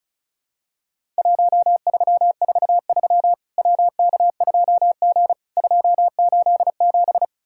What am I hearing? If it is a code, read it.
Morse code, heard as a keyed tone, about 36 wpm, 700 Hz. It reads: 1343 WK2G 287
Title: Worked All Europe DX Contest - QTC Exchange - 35wpm